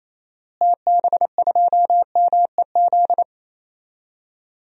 Morse code, heard as a keyed tone, about 28 wpm, 700 Hz. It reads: TB2MEZ
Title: DXCC-100 Call Signs - 6 Characters - Repeated 3x - 28wpm